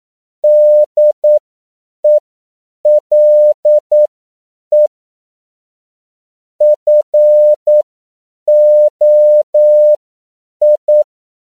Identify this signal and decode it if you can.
Morse code, heard as a keyed tone, about 9 wpm, spaced as for 7 wpm, 600 Hz. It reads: DELE FOI
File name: CW / 10PPM